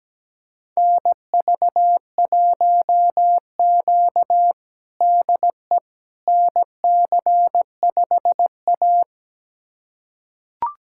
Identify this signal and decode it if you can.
Morse code, heard as a keyed tone, about 17 wpm, 700 Hz. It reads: NV1Q DE NC5A
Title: USA Callsign DE Callsign - 4 Characters - 17wpm